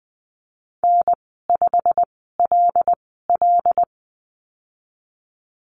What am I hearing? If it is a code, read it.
Morse code, heard as a keyed tone, about 19 wpm, spaced as for 14 wpm, 700 Hz. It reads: N5LL
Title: 128 Lesson 21g - Number 6 - Call Signs 20wpm